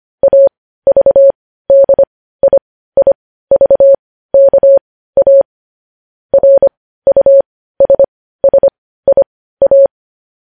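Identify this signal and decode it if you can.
Morse code, heard as a keyed tone, about 25 wpm, spaced as for 15 wpm, 550 Hz. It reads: AVDIIVKA RUSSIA
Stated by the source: Morse code news headlines broadcast.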